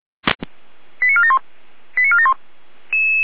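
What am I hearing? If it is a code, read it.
Five-tone selective call (ZVEI), heard as quick runs of five tones, tones of about 70 ms. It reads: 89471 89471
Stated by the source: ZVEI five-tone call received by SDR.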